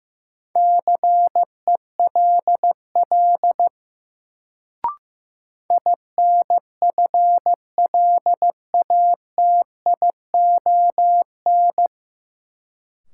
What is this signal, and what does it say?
Morse code, heard as a keyed tone, about 15 wpm, 700 Hz. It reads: CELL INFLATION
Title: Sentences from Top 1100 Words - New Word Drill - 15wpm